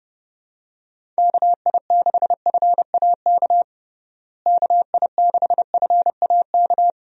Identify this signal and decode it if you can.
Morse code, heard as a keyed tone, about 32 wpm, 700 Hz. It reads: KI6FAK KI6FAK
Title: USA Call Signs - 6 Characters - Repeated 3x - 30wpm